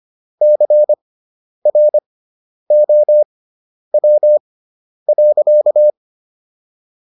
Morse code, heard as a keyed tone, about 25 wpm, spaced as for 10 wpm, 600 Hz. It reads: CROW.